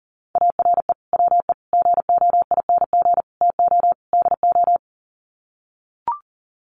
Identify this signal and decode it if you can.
Morse code, heard as a keyed tone, about 40 wpm, 700 Hz. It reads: ARE WE GOING TO DO
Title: Sets of 5 Words 40wpm